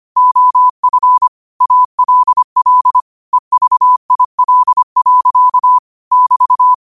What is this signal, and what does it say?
Morse code, heard as a keyed tone, about 25 wpm, 1000 Hz. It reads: OF ALL EVIL. <BT>
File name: qotdfiles / 25wpm